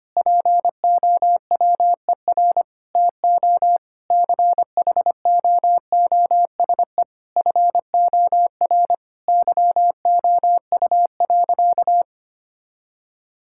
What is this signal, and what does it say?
Morse code, heard as a keyed tone, about 25 wpm, 700 Hz. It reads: POWER TO CHOOSE FOR YOU.
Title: Sentences from Top 1000 Words - Review Part 6 - 25wpm